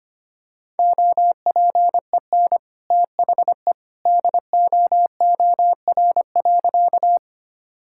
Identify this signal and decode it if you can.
Morse code, heard as a keyed tone, about 25 wpm, 700 Hz. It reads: OPEN THE DOOR.
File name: practice / 25wpm / Sentences with 2-Word Phrases - 25wpm